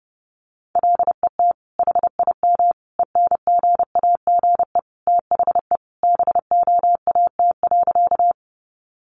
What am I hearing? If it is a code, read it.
Morse code, heard as a keyed tone, about 30 wpm, 700 Hz. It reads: LET HIM ENGAGE THE BOAT.
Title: Sentences from Top 1500 Words 30wpm